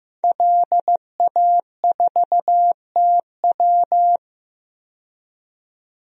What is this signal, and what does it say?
Morse code, heard as a keyed tone, about 15 wpm, 700 Hz. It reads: LA4TW
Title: DXCC-100 Call Signs - 5 Characters - 15wpm